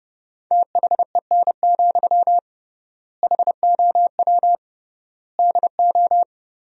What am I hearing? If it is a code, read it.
Morse code, heard as a keyed tone, about 30 wpm, 700 Hz. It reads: THEN, HOW DO
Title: 3x Word-Spacing - Sentences from Top 100 Words - 30wpm